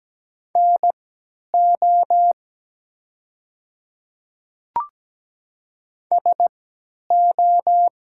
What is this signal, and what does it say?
Morse code, heard as a keyed tone, about 17 wpm, spaced as for 10 wpm, 700 Hz. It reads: NO SO
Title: 3x Character Spacing - Top 100 Words - 17wpm